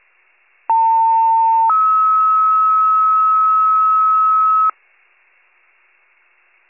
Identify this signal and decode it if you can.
Two-tone page, heard as a one-second tone, then a three-second tone, about 900 then 1290 Hz. It is heard from 0.7 to 4.7 s.
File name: pager / 78 RESCUE DUTY / wcalert